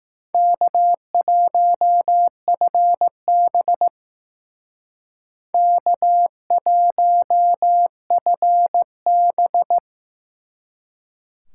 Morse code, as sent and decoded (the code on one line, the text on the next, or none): -.- .---- ..-. -... / -.- .---- ..-. -...
K1FB K1FB